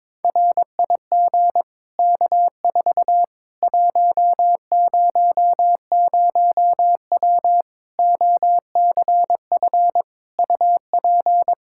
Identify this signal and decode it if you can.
Morse code, heard as a keyed tone, about 21 wpm, 700 Hz. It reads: RIG K4 100W OCF UP
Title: Practice QSO - Exchange 2 - Info - 22wpm